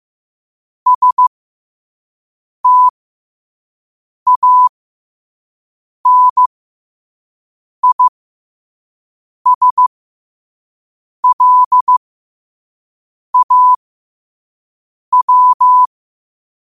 Morse code, heard as a keyed tone, about 15 wpm, spaced as for 5 wpm, 1000 Hz. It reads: STANISLAW
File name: qotdfiles / 5wpm